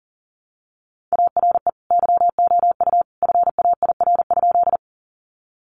Morse code, heard as a keyed tone, about 40 wpm, 700 Hz. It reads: ARE YOU FAIR?
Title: Sentences from Top 1300 Words 40wpm